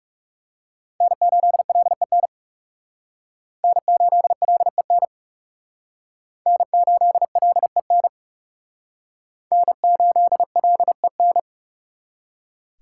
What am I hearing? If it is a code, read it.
Morse code, heard as a keyed tone, about 38 wpm, 700 Hz. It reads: N8LEN N8LEN N8LEN N8LEN